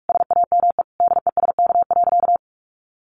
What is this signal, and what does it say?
Morse code, heard as a keyed tone, about 44 wpm, 700 Hz. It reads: SAME DESK.
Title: A Sound Of Thunder 45wpm